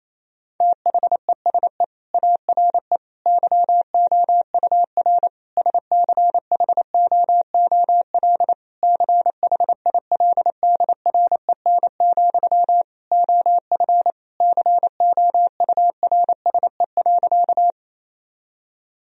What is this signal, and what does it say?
Morse code, heard as a keyed tone, about 28 wpm, 700 Hz. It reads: THESE ARE YOUR SCHOOL CHILDREN, OF COURSE.
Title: Sentences from Top 1000 Words - Review Part 1 - 28wpm